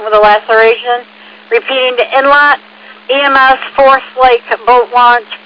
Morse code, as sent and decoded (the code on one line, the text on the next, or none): none